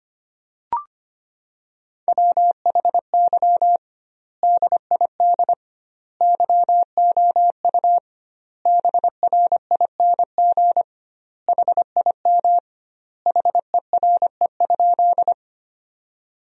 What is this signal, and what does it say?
Morse code, heard as a keyed tone, about 25 wpm, 700 Hz. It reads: WHY DID YOU BRING HIM HERE?
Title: 2x Word-Spacing - Sentences from Top 500 Words - 25wpm